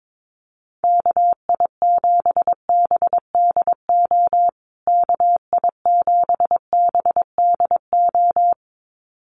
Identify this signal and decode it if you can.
Morse code, heard as a keyed tone, about 22 wpm, 700 Hz. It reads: KI7BDO KI7BDO
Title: USA Call Signs Repeated 2x 22wpm